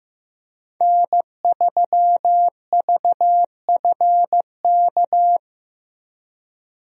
Morse code, heard as a keyed tone, about 15 wpm, 700 Hz. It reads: N3VFK
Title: USA Call Signs - 5 Characters - Repeated 2x - 15wpm